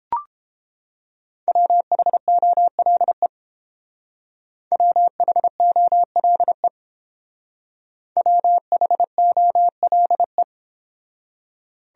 Morse code, heard as a keyed tone, about 29 wpm, 700 Hz. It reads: WHOLE WHOLE WHOLE